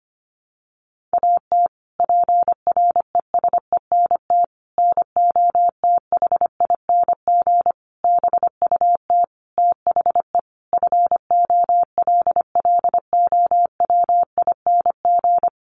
Morse code, heard as a keyed tone, about 25 wpm, 700 Hz. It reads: AT PRESENT NOTHING BUT THE FOLLOWING
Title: Sentences from Top 1500 Words 25wpm